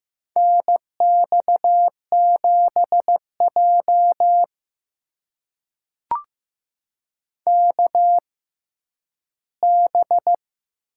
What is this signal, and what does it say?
Morse code, heard as a keyed tone, about 15 wpm, spaced as for 10 wpm, 700 Hz. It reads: NX7J KB
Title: USA Call Signs - 6x Character Spacing - 15wpm